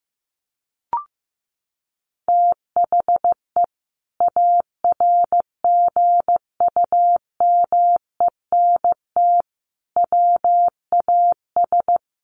Morse code, heard as a keyed tone, about 15 wpm, 700 Hz. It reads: THE ARGUMENT WAS
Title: Sentences from Top 1400 Words 15wpm